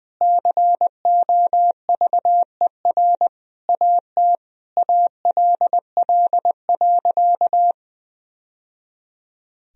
Morse code, heard as a keyed tone, about 20 wpm, 700 Hz. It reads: COVER AT ALL.
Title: Sentences from Top 1000 Words - Review Part 1 - 20wpm